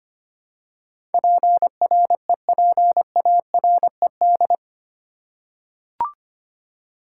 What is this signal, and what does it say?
Morse code, heard as a keyed tone, about 25 wpm, 700 Hz. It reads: PREPARED
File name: 8 Letter Words - First 3 Letters Given - 25wpm